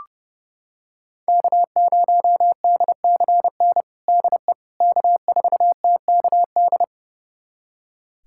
Morse code, heard as a keyed tone, about 30 wpm, 700 Hz. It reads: K0DCN DE K4TKD
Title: USA Callsign DE Callsign - 5 Characters - 30wpm